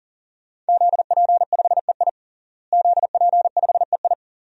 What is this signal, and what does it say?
Morse code, heard as a keyed tone, about 40 wpm, 700 Hz. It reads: ZP5EI ZP5EI
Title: DXCC-100 Call Signs - 5 Characters - Repeated 2x - 40wpm